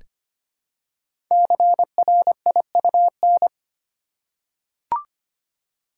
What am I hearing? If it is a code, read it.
Morse code, heard as a keyed tone, about 25 wpm, 700 Hz. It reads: CRIUN
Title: Code Groups - 5 Letters - 25wpm